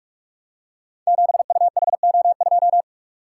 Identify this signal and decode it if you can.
Morse code, heard as a keyed tone, about 44 wpm, 700 Hz. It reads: 7UHO2
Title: Code Groups - 5 Letter:Numbers - 45wpm